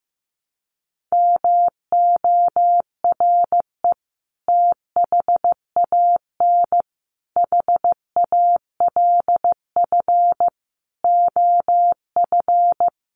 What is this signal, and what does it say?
Morse code, heard as a keyed tone, about 15 wpm, 700 Hz. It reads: MORE THAN HALF OF
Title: Sets of 5 Words 15wpm